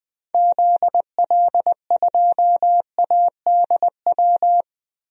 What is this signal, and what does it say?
Morse code, heard as a keyed tone, about 20 wpm, 700 Hz. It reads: ZL2ADW
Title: DXCC-100 Call Signs - 6 Characters - Repeated 3x - 20wpm